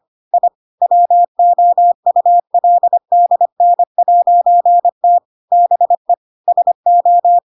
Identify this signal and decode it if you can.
Morse code, heard as a keyed tone, about 25 wpm, 700 Hz. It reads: I WOULDN'T BE SO
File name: Pinocchio / Pinocchio ch5 25 25 ik1hge